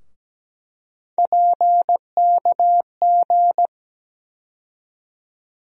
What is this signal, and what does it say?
Morse code, heard as a keyed tone, about 17 wpm, 700 Hz. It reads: PKG
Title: USA Call Sign Postfixes - 3 Letters 17wpm